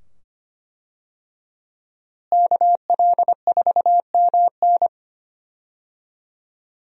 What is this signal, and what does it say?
Morse code, heard as a keyed tone, about 25 wpm, 700 Hz. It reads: KL4MN
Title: USA Call Signs - 5 Characters - Repeated 2x - 25wpm